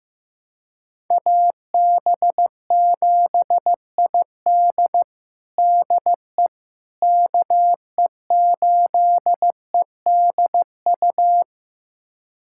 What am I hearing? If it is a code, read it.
Morse code, heard as a keyed tone, about 15 wpm, 700 Hz. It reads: AB7ID DE KE8EDU
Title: USA Callsign DE Callsign 15wpm